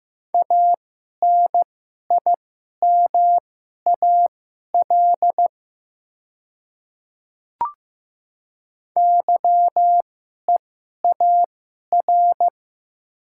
Morse code, heard as a keyed tone, about 15 wpm, spaced as for 11 wpm, 700 Hz. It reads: ANIMAL YEAR
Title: 2x Character Spacing - Top 200 Words - 15wpm